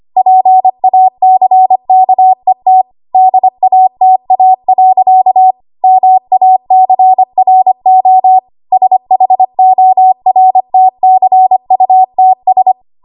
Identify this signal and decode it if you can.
Morse code, heard as a keyed tone, about 25 wpm, 750 Hz. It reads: PACKET DATA. MACRO SHORTCUTS
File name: Morse / 25WPM